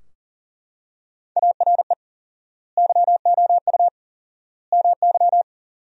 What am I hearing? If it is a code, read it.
Morse code, heard as a keyed tone, about 40 wpm, 700 Hz. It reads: ARE YOU MY
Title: 4x Word-Spacing - Sentences from Top 300 Words - 40wpm